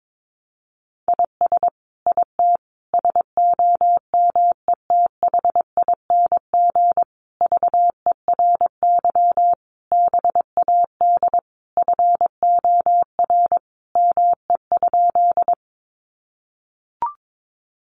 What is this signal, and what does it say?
Morse code, heard as a keyed tone, about 22 wpm, 700 Hz. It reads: IS IT SOMETHING VERY BAD FOR ME?
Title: Sentences from Top 700 Words 22wpm